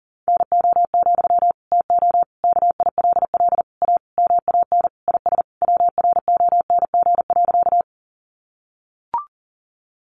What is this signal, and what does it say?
Morse code, heard as a keyed tone, about 40 wpm, 700 Hz. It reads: NO, TO KILL A MAN IS WRONG.
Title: Sentences from Top 1400 Words 40wpm